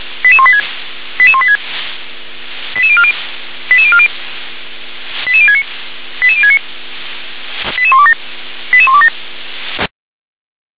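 Five-tone selective call (ZVEI), heard as quick runs of five tones, tones of about 70 ms. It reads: 88186 88186 88040 88040 88069 88069 88127 88127
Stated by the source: Received on SDR, ZVEI five-tone selective call 88103.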